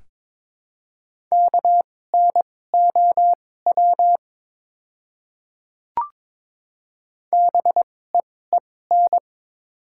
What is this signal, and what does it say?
Morse code, heard as a keyed tone, about 22 wpm, spaced as for 16 wpm, 700 Hz. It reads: KNOW BEEN
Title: 2x Character Spacing - Top 200 Words - 22wpm